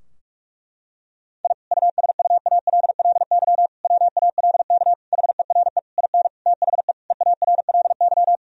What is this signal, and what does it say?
Morse code, heard as a keyed tone, about 43 wpm, 700 Hz. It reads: I USUALLY WALK HERE IN THE EARLY
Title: Sentences from Top 1000 Words - Review Part 3 - 45wpm